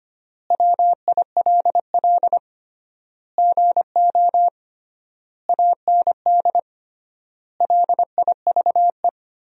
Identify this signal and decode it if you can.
Morse code, heard as a keyed tone, about 25 wpm, 700 Hz. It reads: WILL GO AND LIVE
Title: 3x Word-Spacing - Sentences from Top 500 Words - 25wpm